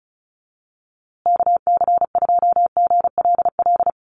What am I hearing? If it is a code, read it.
Morse code, heard as a keyed tone, about 34 wpm, 700 Hz. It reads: KC2GLL
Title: USA Call Signs Repeated 3x 35wpm